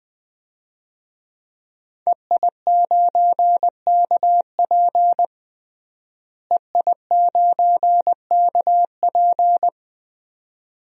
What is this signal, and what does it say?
Morse code, heard as a keyed tone, about 20 wpm, 700 Hz. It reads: EI9KP EI9KP
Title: DXCC-100 Call Signs - 5 Characters - Repeated 2x - 20wpm